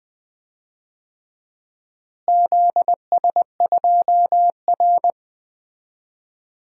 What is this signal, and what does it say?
Morse code, heard as a keyed tone, about 20 wpm, 700 Hz. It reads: ZS2R
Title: DXCC-100 Call Signs - 4 Characters - Repeated 3x - 20wpm